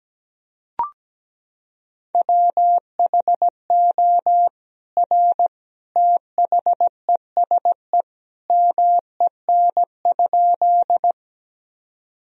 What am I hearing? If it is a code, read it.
Morse code, heard as a keyed tone, about 17 wpm, 700 Hz. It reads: WHO R THESE MEN?
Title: Sentences from Top 200 Words and Abbreviations 17wpm